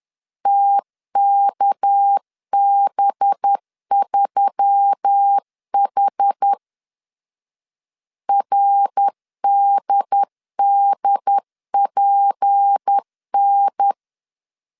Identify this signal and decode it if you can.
Morse code, heard as a keyed tone, about 10 wpm, 800 Hz. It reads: TKB3H RDDPN